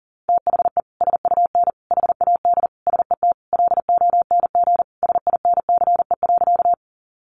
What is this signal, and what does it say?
Morse code, heard as a keyed tone, about 40 wpm, 700 Hz. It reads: THE SUN HAD SET LONG SINCE.
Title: Sentences from Top 300 Words 40wpm